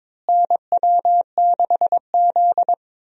Morse code, heard as a keyed tone, about 22 wpm, 700 Hz. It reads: NW6Z
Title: USA Call Signs - 4 Characters - Repeated 3x - 22wpm